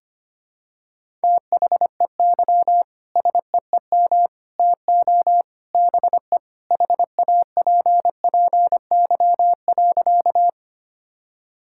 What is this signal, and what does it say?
Morse code, heard as a keyed tone, about 25 wpm, 700 Hz. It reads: THEY SEEM TO BE HAPPY.
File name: practice / 25wpm / Sentences with 3-Word Phrases - 25wpm